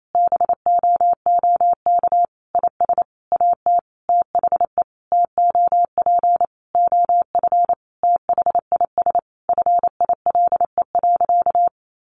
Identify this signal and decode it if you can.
Morse code, heard as a keyed tone, about 28 wpm, 700 Hz. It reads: BOOK IS AT THE TOP OF THIS FILE.
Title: Sentences from Top 1200 Words 28wpm